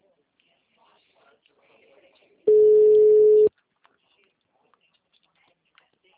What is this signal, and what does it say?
Ringback tone (Europe, 425 Hz), with speech.
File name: out- -2002- - - .52271